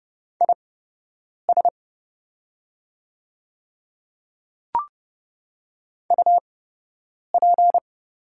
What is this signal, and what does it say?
Morse code, heard as a keyed tone, about 29 wpm, spaced as for 8 wpm, 700 Hz. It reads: IS UP
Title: 8x Character Spacing - Top 100 Words - 30wpm